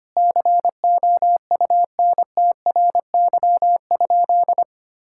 Morse code, heard as a keyed tone, about 25 wpm, 700 Hz. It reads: COUNTRY?
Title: Sentences from Top 1000 Words - Review Part 5 - 25wpm